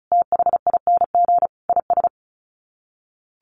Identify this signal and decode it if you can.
Morse code, heard as a keyed tone, about 34 wpm, 700 Hz. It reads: THING IS
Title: Sets of 5 Words Encore 35wpm